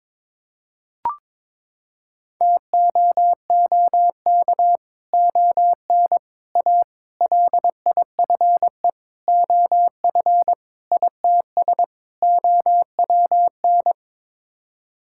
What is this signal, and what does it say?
Morse code, heard as a keyed tone, about 22 wpm, 700 Hz. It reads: TOOK ON A LIFE OF ITS OWN
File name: practice / 22wpm / Sets of 7 Words - 22wpm